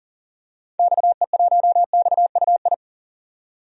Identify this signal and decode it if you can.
Morse code, heard as a keyed tone, about 40 wpm, 700 Hz. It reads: XE1XUI
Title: DXCC-100 Call Signs - 6 Characters - 40wpm